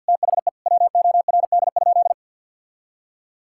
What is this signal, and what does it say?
Morse code, heard as a keyed tone, about 50 wpm, 700 Hz. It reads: THE WORD?
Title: Sentences from Top 1000 Words - Review Part 6 - 50wpm